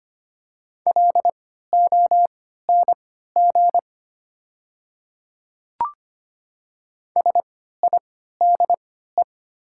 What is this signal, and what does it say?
Morse code, heard as a keyed tone, about 25 wpm, spaced as for 14 wpm, 700 Hz. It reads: LONG SIDE
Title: 3x Character Spacing - Top 100 Words - 25wpm